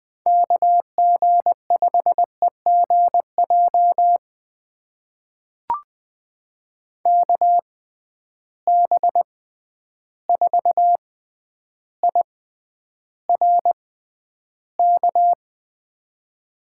Morse code, heard as a keyed tone, about 21 wpm, spaced as for 10 wpm, 700 Hz. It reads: KG5EGJ KB4IRK
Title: USA Call Signs - 6 Characters - 6x Character Spacing - 20wpm